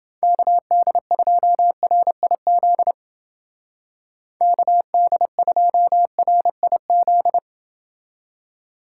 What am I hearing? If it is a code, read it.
Morse code, heard as a keyed tone, about 29 wpm, 700 Hz. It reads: KD2RIZ KD2RIZ